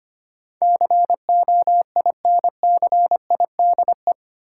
Morse code, heard as a keyed tone, about 25 wpm, 700 Hz. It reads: COINCIDE
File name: 8 Letter Words - First 3 Letters Given - 25wpm